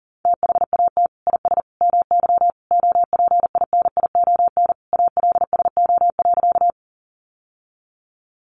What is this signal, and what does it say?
Morse code, heard as a keyed tone, about 40 wpm, 700 Hz. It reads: THAT IS MY OPINION ALSO.
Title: Sentences from Top 1200 Words 40wpm